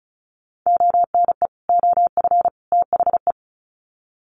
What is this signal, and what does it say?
Morse code, heard as a keyed tone, about 34 wpm, 700 Hz. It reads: ONE OF THE
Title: Sets of 3 Words 35wpm